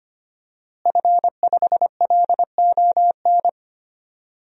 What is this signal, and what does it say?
Morse code, heard as a keyed tone, about 25 wpm, 700 Hz. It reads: F5LON
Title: DXCC-100 Call Signs - 5 Characters - Repeated 2x - 25wpm